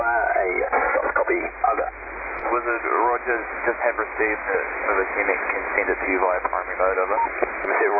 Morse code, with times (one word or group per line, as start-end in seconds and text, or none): none